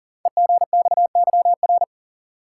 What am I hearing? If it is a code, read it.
Morse code, heard as a keyed tone, about 40 wpm, 700 Hz. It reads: EGXYR
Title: Code Groups - 5 Letters - 40wpm